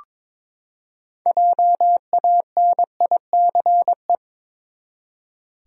Morse code, heard as a keyed tone, about 21 wpm, 700 Hz. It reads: JANICE
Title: USA Common Names - 22wpm